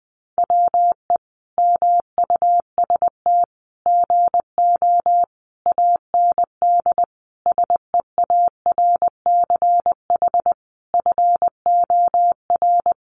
Morse code, heard as a keyed tone, about 20 wpm, 700 Hz. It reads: WE MUST GO AND SEARCH FOR
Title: Sentences from Top 800 Words 20wpm